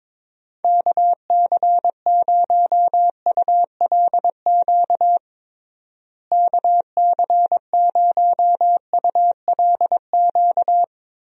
Morse code, heard as a keyed tone, about 22 wpm, 700 Hz. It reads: KC0ULQ KC0ULQ
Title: USA Call Signs - 6 Characters - Repeated 3x - 22wpm